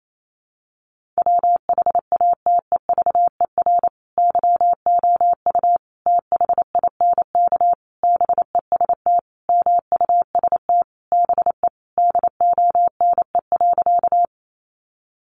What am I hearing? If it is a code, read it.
Morse code, heard as a keyed tone, about 28 wpm, 700 Hz. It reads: WHATEVER YOU THINK BEST MUST BE DONE.
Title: Sentences from Top 1100 Words 28wpm